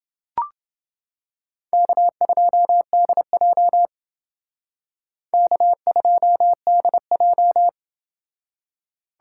Morse code, heard as a keyed tone, about 28 wpm, 700 Hz. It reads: K2DJ K2DJ